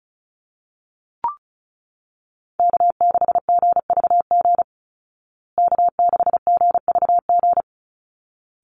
Morse code, heard as a keyed tone, about 34 wpm, 700 Hz. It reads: K6GVG K6GVG
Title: Call Signs Repeated 3x - 35wpm